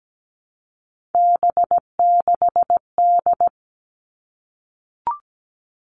Morse code, heard as a keyed tone, about 17 wpm, 700 Hz. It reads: B6D
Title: All Cut Numbers - 3-Digit 17wpm